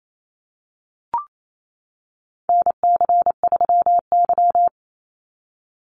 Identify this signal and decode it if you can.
Morse code, heard as a keyed tone, about 28 wpm, 700 Hz. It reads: NC3Y
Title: Call Signs - 28wpm